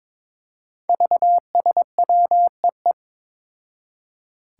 Morse code, heard as a keyed tone, about 22 wpm, 700 Hz. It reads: VSWEE
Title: Five Challenging Letter-Number Combinations 22wpm